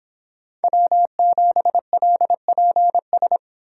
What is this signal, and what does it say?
Morse code, heard as a keyed tone, about 31 wpm, 700 Hz. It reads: W7LPS